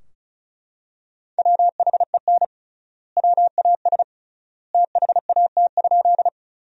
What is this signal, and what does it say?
Morse code, heard as a keyed tone, about 36 wpm, 700 Hz. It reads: WHEN WAS THAT?
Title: 3x Word-Spacing - Sentences from Top 100 Words - 35wpm